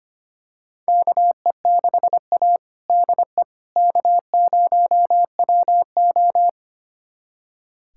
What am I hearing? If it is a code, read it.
Morse code, heard as a keyed tone, about 25 wpm, 700 Hz. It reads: KE6A DE K0WO
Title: USA Callsign DE Callsign - 4 Characters - 25wpm